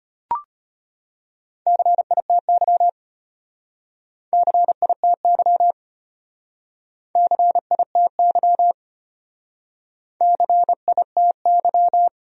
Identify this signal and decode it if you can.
Morse code, heard as a keyed tone, about 32 wpm, 700 Hz. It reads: CITY CITY CITY CITY